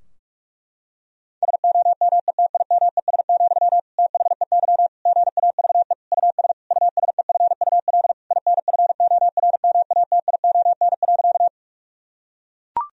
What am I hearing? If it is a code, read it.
Morse code, heard as a keyed tone, about 41 wpm, 700 Hz. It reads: SOMETIMES, THEY GAVE US USEFUL INFORMATION.
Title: Sentences from Top 1000 Words - Review Part 4 - 45wpm